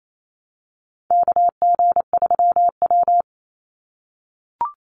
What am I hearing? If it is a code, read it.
Morse code, heard as a keyed tone, about 28 wpm, 700 Hz. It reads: KG3W
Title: NA SOTA Chaser Callsigns 28wpm